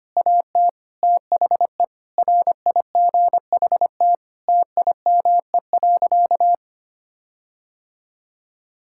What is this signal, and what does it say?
Morse code, heard as a keyed tone, about 25 wpm, 700 Hz. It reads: AT THE RIGHT TIME.
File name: practice / 25wpm / Sentences with 9-Word Phrases - 25wpm